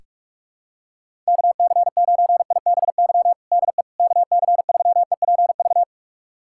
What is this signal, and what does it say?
Morse code, heard as a keyed tone, about 47 wpm, 700 Hz. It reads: KC9IBY DE KC3EPV
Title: USA Callsign DE Callsign - 6 Characters - 45wpm